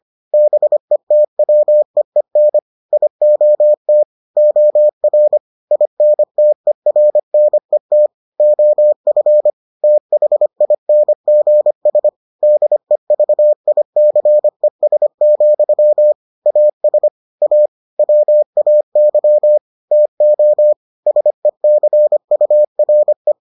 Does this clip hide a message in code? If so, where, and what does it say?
Yes: Morse code from 0.3 to 23.3 s, heard as a keyed tone, about 25 wpm, 600 Hz. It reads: BETWEEN IOT OR INTERNET OF THINGS DEVICES, AS A WAY TO SECURE